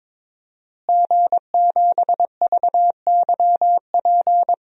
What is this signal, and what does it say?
Morse code, heard as a keyed tone, about 21 wpm, 700 Hz. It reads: G7VYP
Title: DXCC-100 Call Signs - 5 Characters - Repeated 2x - 22wpm